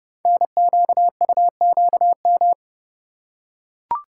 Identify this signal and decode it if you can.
Morse code, heard as a keyed tone, about 30 wpm, 700 Hz. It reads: NQUQM
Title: Code Groups - 5 Letters - 30wpm